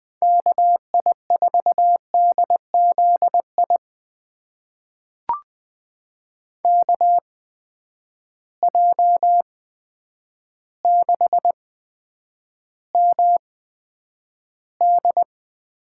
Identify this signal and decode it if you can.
Morse code, heard as a keyed tone, about 20 wpm, spaced as for 9 wpm, 700 Hz. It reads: KI4DZI KJ6MD
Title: USA Call Signs - 6 Characters - 8x Character Spacing - 20wpm